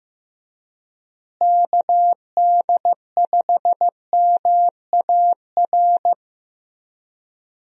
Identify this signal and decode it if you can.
Morse code, heard as a keyed tone, about 16 wpm, 700 Hz. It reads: KD5MAR
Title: USA Call Signs - 6 Characters - Repeated 3x - 15wpm